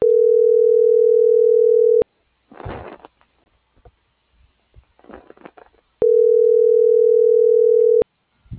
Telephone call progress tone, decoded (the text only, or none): ringback tone